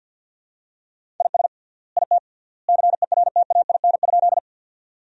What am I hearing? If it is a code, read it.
Morse code, heard as a keyed tone, about 47 wpm, 700 Hz. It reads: IS IT CERTAIN?
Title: 3x Word-Spacing - Sentences from Top 500 Words - 50wpm